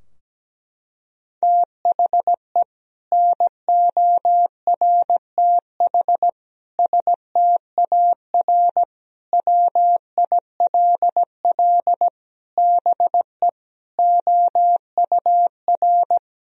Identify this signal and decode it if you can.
Morse code, heard as a keyed tone, about 17 wpm, 700 Hz. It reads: THE NORTH STAR WILL BE OUR
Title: Sentences from Top 1000 Words - Review Part 3 - 17wpm